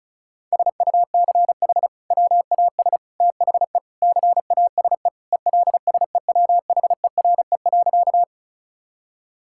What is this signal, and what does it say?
Morse code, heard as a keyed tone, about 36 wpm, 700 Hz. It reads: SUCH WAS THE CASE ELSEWHERE.
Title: Sentences from Top 1600 Words 35wpm